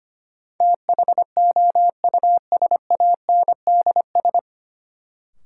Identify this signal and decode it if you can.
Morse code, heard as a keyed tone, about 25 wpm, 700 Hz. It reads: THOUSANDS
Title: Sets of 4 Words - New Word Drill - 25wpm